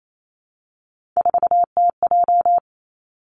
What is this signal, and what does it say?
Morse code, heard as a keyed tone, about 28 wpm, 700 Hz. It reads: 4TJ
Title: Three Challenging Letter-Number Combinations 28wpm